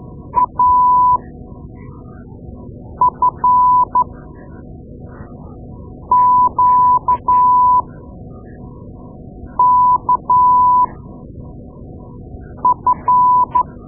Morse code, heard as a keyed tone, about 13 wpm, spaced as for 4 wpm, 1000 Hz. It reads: AFQKF
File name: Morse Lessons / Morse Lessons / Tape-2A